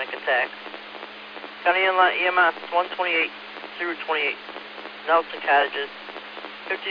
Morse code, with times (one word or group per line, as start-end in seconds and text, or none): none